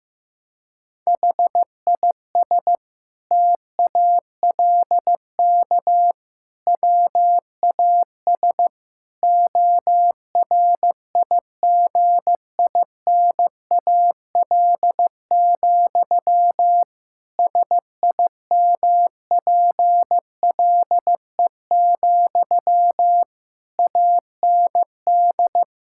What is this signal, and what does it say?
Morse code, heard as a keyed tone, about 16 wpm, 700 Hz. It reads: HIS TALK WAS ORIGINAL, SIMPLE, AND
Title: Sentences from Top 1000 Words - Review Part 6 - 15wpm